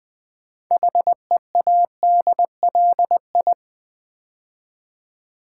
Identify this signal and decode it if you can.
Morse code, heard as a keyed tone, about 20 wpm, 700 Hz. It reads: HEADLI
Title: Build A Word - 2 Letters At A Time - 8-Letter Words - 20wpm